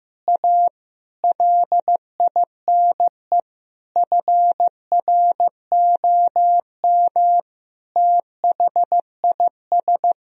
Morse code, heard as a keyed tone, about 15 wpm, 700 Hz. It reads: A LINE FROM THIS
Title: Sentences from Top 1000 Words - Review Part 1 - 15wpm